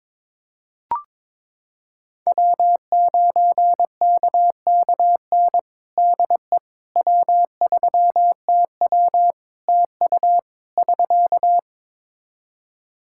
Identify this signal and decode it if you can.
Morse code, heard as a keyed tone, about 22 wpm, 700 Hz. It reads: W9KKN DE W3TW TU <SK>
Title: Practice QSO - Exchange 4 - Ending - 22wpm